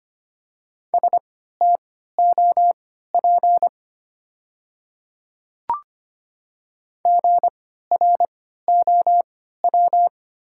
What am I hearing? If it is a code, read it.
Morse code, heard as a keyed tone, about 25 wpm, spaced as for 14 wpm, 700 Hz. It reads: STOP GROW
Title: 3x Character Spacing - Top 300 Words - 25wpm